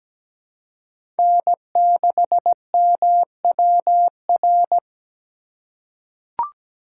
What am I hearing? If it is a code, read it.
Morse code, heard as a keyed tone, about 17 wpm, 700 Hz. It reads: N6MWR
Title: USA Call Signs - 5 Characters - 4x Character Spacing - 17wpm